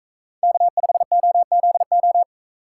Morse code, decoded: K5OZO

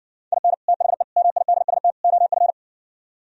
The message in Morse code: IN THE MIDST OF

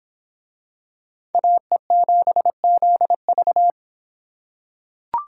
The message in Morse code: AE7ZV

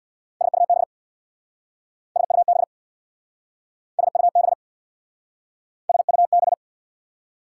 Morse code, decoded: SUB SUB SUB SUB

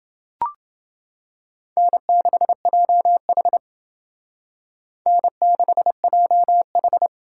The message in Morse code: N6JH N6JH

